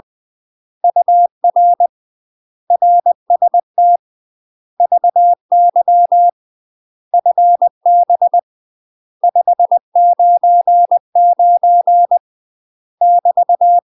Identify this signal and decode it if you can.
Morse code, heard as a keyed tone, about 20 wpm, 700 Hz. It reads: UR RST VY FB 599 <BT>
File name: UK MASTER QSOs TRAINING / 20WPM / UK MASTER QSOs 0037